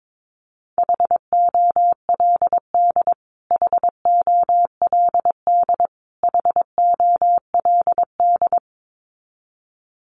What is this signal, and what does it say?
Morse code, heard as a keyed tone, about 22 wpm, 700 Hz. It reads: HOLD HOLD HOLD
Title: Top 500 Words Repeated 22wpm